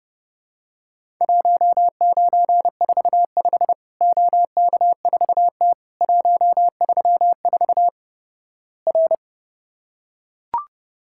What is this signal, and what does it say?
Morse code, heard as a keyed tone, about 30 wpm, 700 Hz. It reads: 1945 OK4T 134 R
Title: Worked All Europe DX Contest - QTC Exchange - 30wpm